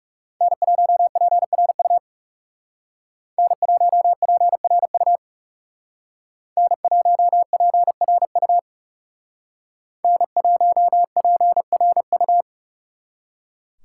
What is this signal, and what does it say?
Morse code, heard as a keyed tone, about 36 wpm, 700 Hz. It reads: N1PRU N1PRU N1PRU N1PRU
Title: Speed Racing - USA 1x3 Call Signs 30wpm